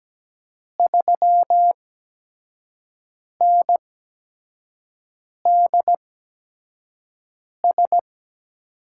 Morse code, heard as a keyed tone, about 17 wpm, spaced as for 5 wpm, 700 Hz. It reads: 3NDS
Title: USA Call Signs - 6 Characters - 8x Character Spacing - 17wpm